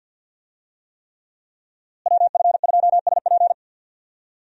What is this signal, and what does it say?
Morse code, heard as a keyed tone, about 52 wpm, 700 Hz. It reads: WF2SP
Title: USA Call Signs - 5 Characters - Repeated 3x - 50wpm